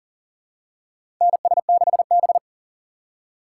Morse code, heard as a keyed tone, about 40 wpm, 700 Hz. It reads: NS6B
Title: USA Call Signs - 4 Characters - 3x Character Spacing - 40wpm